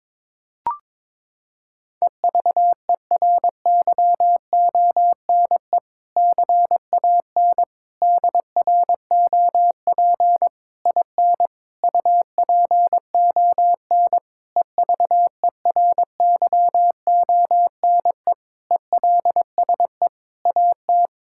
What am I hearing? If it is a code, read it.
Morse code, heard as a keyed tone, about 22 wpm, 700 Hz. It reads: EVERYONE CAN DROP IN UPON EVERYONE ELSE AT
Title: Sentences from Top 1600 Words 22wpm